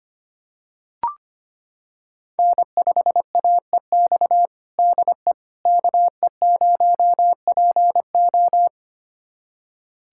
Morse code, heard as a keyed tone, about 25 wpm, 700 Hz. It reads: N5AEX DE KE0PO
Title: USA Callsign DE Callsign - 5 Characters - 25wpm